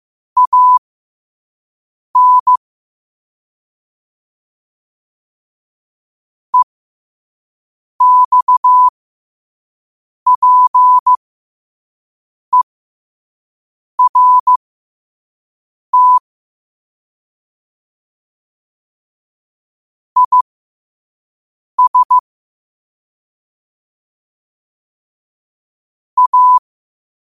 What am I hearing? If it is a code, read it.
Morse code, heard as a keyed tone, about 15 wpm, spaced as for 5 wpm, 1000 Hz. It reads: AN EXPERT IS A
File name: qotdfiles / 5wpm